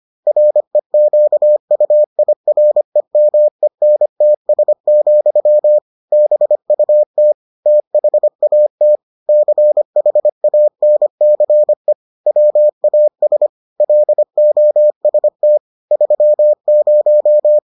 Morse code, heard as a keyed tone, about 25 wpm, 600 Hz. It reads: REQUIREMENTS, BUT THAT CHANCE WAS LOST 30